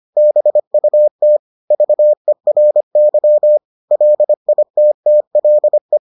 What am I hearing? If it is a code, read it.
Morse code, heard as a keyed tone, about 25 wpm, 600 Hz. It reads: BUT VERY LITTLE